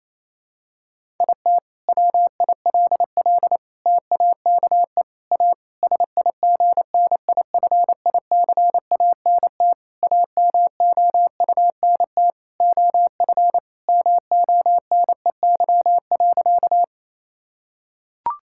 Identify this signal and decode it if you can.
Morse code, heard as a keyed tone, about 28 wpm, 700 Hz. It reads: IT WILL TAKE A SIGNIFICANT AMOUNT OF MONEY.
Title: Sentences from Top 1000 Words - Review Part 6 - 28wpm